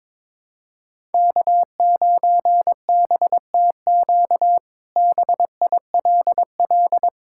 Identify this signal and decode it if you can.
Morse code, heard as a keyed tone, about 21 wpm, 700 Hz. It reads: K9BTQ BILL
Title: K1USN SST QSO Practice - 22wpm